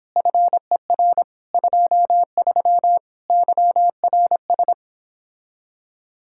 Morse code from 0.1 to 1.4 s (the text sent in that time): FER